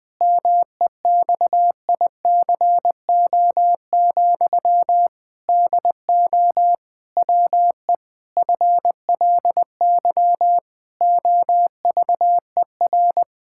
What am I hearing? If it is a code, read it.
Morse code, heard as a keyed tone, about 20 wpm, 700 Hz. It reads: MEXICO, DO WE FLY OVER
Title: Pangram Challenge v2 - 20wpm